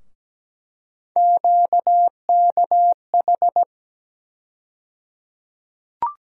Morse code, heard as a keyed tone, about 17 wpm, 700 Hz. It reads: QKH
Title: USA Call Sign Postfixes - 3 Letters 17wpm